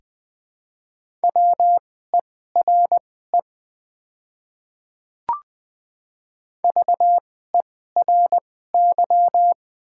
Morse code, heard as a keyed tone, about 20 wpm, spaced as for 14 wpm, 700 Hz. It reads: WERE VERY
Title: 2x Character Spacing - Top 300 Words - 20wpm